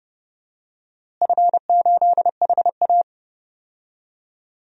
Morse code, heard as a keyed tone, about 32 wpm, 700 Hz. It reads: F8HA
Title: DXCC-100 Call Signs - 4 Characters - Repeated 2x - 30wpm